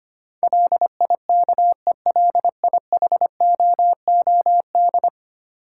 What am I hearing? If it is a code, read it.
Morse code, heard as a keyed tone, about 25 wpm, 700 Hz. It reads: LIKELIHOOD
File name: Ten Letter Words - 25wpm